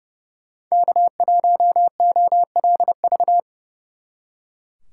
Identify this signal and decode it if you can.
Morse code, heard as a keyed tone, about 36 wpm, 700 Hz. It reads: K1OLV
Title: Speed Racing - USA 1x3 Call Signs 30wpm